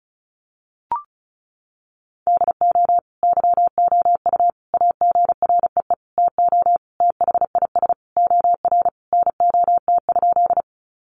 Morse code, heard as a keyed tone, about 34 wpm, 700 Hz. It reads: DO YOU AGREE TO THIS OR NOT?
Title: Sentences from Top 1100 Words 35wpm